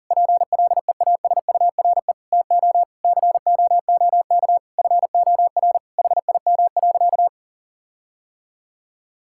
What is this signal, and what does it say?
Morse code, heard as a keyed tone, about 40 wpm, 700 Hz. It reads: PLEASURE TO COOK FOR HIM.
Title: Sentences from Top 1600 Words 40wpm